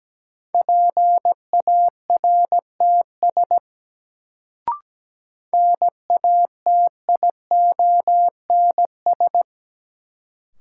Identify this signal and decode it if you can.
Morse code, heard as a keyed tone, about 17 wpm, 700 Hz. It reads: PARTS NATIONS
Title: Sets of 3 Words - New Word Drill - 17wpm